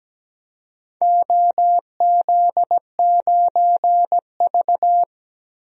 Morse code, heard as a keyed tone, about 17 wpm, 700 Hz. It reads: OZ9V
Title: DXCC-100 Call Signs - 4 Characters - 17wpm